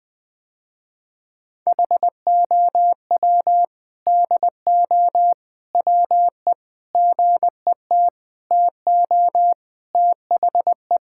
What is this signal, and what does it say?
Morse code, heard as a keyed tone, about 21 wpm, 700 Hz. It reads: HOW DO WE GET TO THE
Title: Sentences with 4-Word Phrases - Spoken Beginning - 20wpm